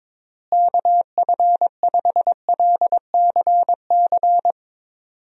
Morse code, heard as a keyed tone, about 22 wpm, 700 Hz. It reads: KF5LCC
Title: USA Call Signs - 6 Characters - Repeated 2x - 22wpm